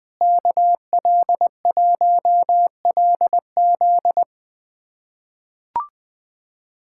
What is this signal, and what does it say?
Morse code, heard as a keyed tone, about 20 wpm, 700 Hz. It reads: KL1LZ
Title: USA Callsign DE Callsign - 5 Characters - 20wpm